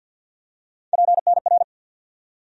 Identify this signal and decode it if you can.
Morse code, heard as a keyed tone, about 47 wpm, 700 Hz. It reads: PNR
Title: USA Call Sign Postfixes - 3 Letters 50wpm